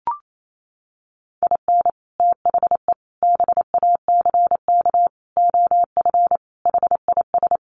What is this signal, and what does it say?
Morse code, heard as a keyed tone, about 28 wpm, 700 Hz. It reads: IN THE BACK OF HIS
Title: Sets of 5 Words Encore2 28wpm